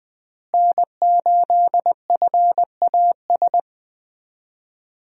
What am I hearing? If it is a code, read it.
Morse code, heard as a keyed tone, about 20 wpm, 700 Hz. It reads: N8FAS
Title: USA Call Signs - 5 Characters - Repeated 2x - 20wpm